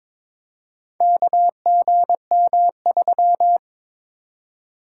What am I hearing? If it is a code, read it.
Morse code, heard as a keyed tone, about 23 wpm, 700 Hz. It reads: KGM3
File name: practice / 22wpm / Four Letter-Number Combinations - 22wpm